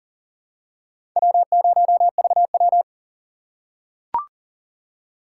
Morse code, heard as a keyed tone, about 40 wpm, 700 Hz. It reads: W0VW
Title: USA Call Signs - 4 Characters - 4x Character Spacing - 40wpm